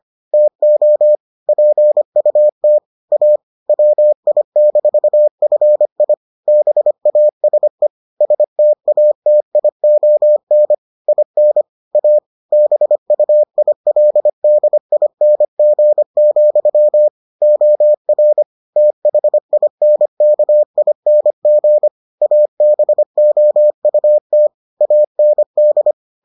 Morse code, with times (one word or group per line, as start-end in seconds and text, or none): 0.3-1.1 TO
1.5-2.8 PUT
3.1-3.4 A
3.7-6.1 WI-FI
6.5-7.9 BASE
8.2-10.7 STATION
11.1-11.6 IN
12.0-12.2 A
12.5-17.1 BUILDING,
17.4-18.4 OR
18.8-21.9 THINKING
22.2-24.5 ABOUT
24.8-25.9 AND